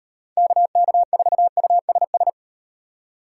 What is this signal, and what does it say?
Morse code, heard as a keyed tone, about 38 wpm, 700 Hz. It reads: KK4USS